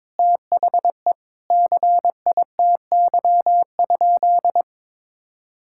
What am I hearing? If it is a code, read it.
Morse code, heard as a keyed tone, about 21 wpm, 700 Hz. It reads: THE CITY?
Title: Sentences from Top 1000 Words - Review Part 5 - 22wpm